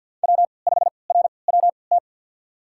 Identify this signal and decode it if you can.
Morse code, heard as a keyed tone, about 50 wpm, spaced as for 28 wpm, 700 Hz. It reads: W5RWT
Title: USA Call Signs - 5 Characters - 3x Character Spacing - 50wpm